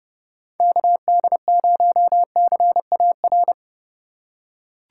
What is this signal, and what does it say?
Morse code, heard as a keyed tone, about 30 wpm, 700 Hz. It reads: KD0CAR
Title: USA Call Signs - 6 Characters - 2x Character Spacing - 30wpm